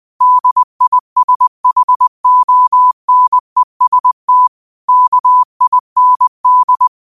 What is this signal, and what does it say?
Morse code, heard as a keyed tone, about 20 wpm, 1000 Hz. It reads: DISHONEST KIND